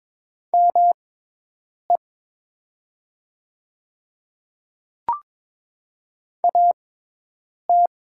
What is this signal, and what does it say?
Morse code, heard as a keyed tone, about 22 wpm, spaced as for 8 wpm, 700 Hz. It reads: ME AT
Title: 6x Character Spacing - Top 200 Words - 22wpm